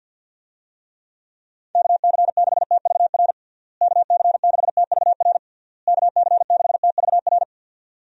Morse code, heard as a keyed tone, about 51 wpm, 700 Hz. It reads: KC6TVR KC6TVR KC6TVR
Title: USA Call Signs - 6 Characters - Repeated 3x - 50wpm